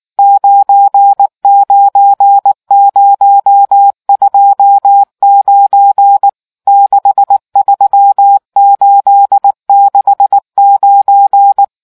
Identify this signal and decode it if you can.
Morse code, heard as a keyed tone, about 19 wpm, 800 Hz. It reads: 99029 63869